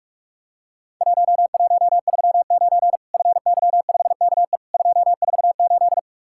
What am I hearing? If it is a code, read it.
Morse code, heard as a keyed tone, about 48 wpm, 700 Hz. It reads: 1139 FY5KE 248
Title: Worked All Europe DX Contest - QTC Exchange - 45wpm